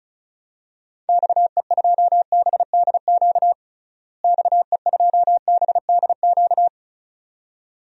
Morse code, heard as a keyed tone, about 34 wpm, 700 Hz. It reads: XE2BDQ XE2BDQ
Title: DXCC-100 Call Signs - 6 Characters - Repeated 2x - 35wpm